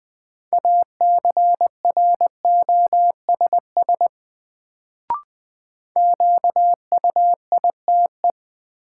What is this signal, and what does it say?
Morse code, heard as a keyed tone, about 20 wpm, 700 Hz. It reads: ACROSS QUITE
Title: Sentences from Top 700 Words - New Word Drill - 20wpm